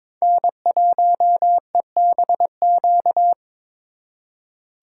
Morse code, heard as a keyed tone, about 26 wpm, 700 Hz. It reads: N1EBQ
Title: Speed Racing - USA 1x3 Call Signs 22wpm